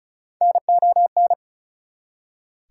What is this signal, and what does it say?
Morse code, heard as a keyed tone, about 36 wpm, 700 Hz. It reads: NON
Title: Prefixes and Suffixes 35wpm